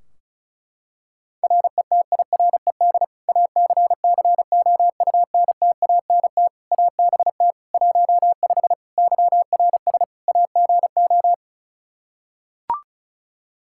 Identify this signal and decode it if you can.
Morse code, heard as a keyed tone, about 36 wpm, 700 Hz. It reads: RETIRED ACCOUNTANT ABT 15 YRS AGO
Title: Practice QSO - Exchange 3 - More Info - 35wpm